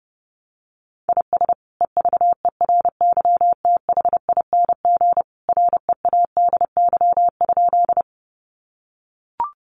Morse code, heard as a keyed tone, about 30 wpm, 700 Hz. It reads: IS EVERYTHING READY?
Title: Sentences from Top 700 Words 30wpm